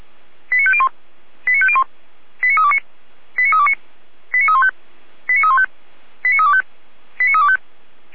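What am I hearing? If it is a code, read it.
Five-tone selective call (ZVEI), heard as quick runs of five tones, tones of about 70 ms. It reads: 89501 89501 89329 89329 89315 89315 89325 89325
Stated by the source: ZVEI five-tone selective call 89376 received by SDR.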